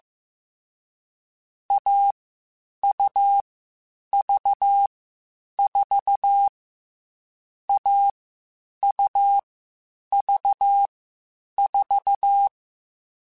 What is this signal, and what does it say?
Morse code, heard as a keyed tone, about 15 wpm, 800 Hz. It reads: A U V 4 A U V 4